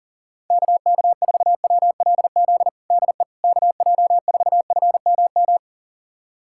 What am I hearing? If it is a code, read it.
Morse code, heard as a keyed tone, about 40 wpm, 700 Hz. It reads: KK4WLZ DE KJ4FMM
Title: USA Callsign DE Callsign - 6 Characters - 40wpm